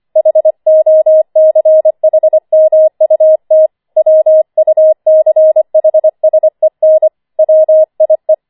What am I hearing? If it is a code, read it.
Morse code, heard as a keyed tone, about 25 wpm, 630 Hz. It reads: HOCHMUT WUCHSEN WIE
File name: 25wpm